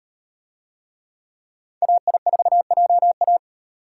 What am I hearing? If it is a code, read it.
Morse code, heard as a keyed tone, about 38 wpm, 700 Hz. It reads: AI4JA